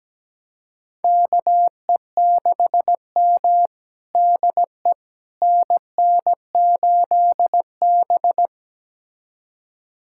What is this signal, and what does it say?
Morse code, heard as a keyed tone, about 17 wpm, 700 Hz. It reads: KE6M DE NN8B
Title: USA Callsign DE Callsign - 4 Characters - 17wpm